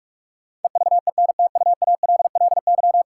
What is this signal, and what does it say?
Morse code, heard as a keyed tone, about 46 wpm, 700 Hz. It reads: EVENTUALLY
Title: Sentences from Top 1000 Words - Review Part 6 - New Word Drill - 45wpm